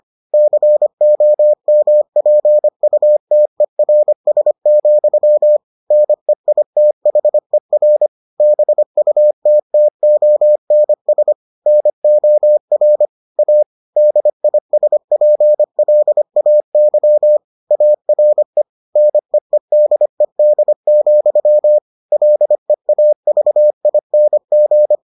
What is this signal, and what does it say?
Morse code, heard as a keyed tone, about 25 wpm, 600 Hz. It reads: COMPUTERS, NEITHER BUTTONS NOR A DISPLAY ARE NEEDED, LEAVING